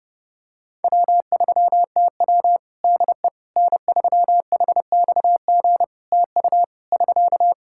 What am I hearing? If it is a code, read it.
Morse code, heard as a keyed tone, about 30 wpm, 700 Hz. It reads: W3TW DE N3HXG TU <SK>
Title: Practice QSO - Exchange 4 - Ending - 30wpm